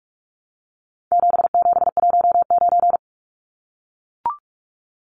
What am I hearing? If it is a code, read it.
Morse code, heard as a keyed tone, about 44 wpm, 700 Hz. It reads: 7719
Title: Numbers - 4-Digit 45wpm